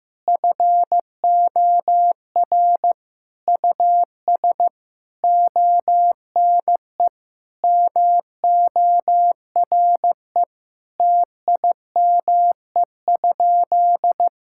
Morse code, heard as a keyed tone, about 15 wpm, 700 Hz. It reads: FOR US ONE MORE TIME?
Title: Sentences from Top 1000 Words - Review Part 5 - 15wpm